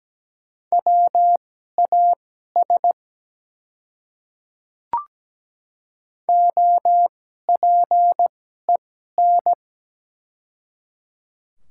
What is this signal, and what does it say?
Morse code, heard as a keyed tone, about 17 wpm, spaced as for 12 wpm, 690 Hz. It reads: WAS OPEN
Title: 2x Character Spacing - Top 300 Words - 17wpm